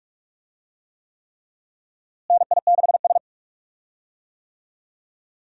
Morse code, heard as a keyed tone, about 46 wpm, 700 Hz. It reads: NI6S
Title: USA Call Signs - 4 Characters - Repeated 3x - 45wpm